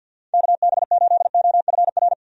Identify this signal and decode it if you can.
Morse code, heard as a keyed tone, about 47 wpm, 700 Hz. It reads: KB8OFR
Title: USA Call Signs - 6 Characters - 50wpm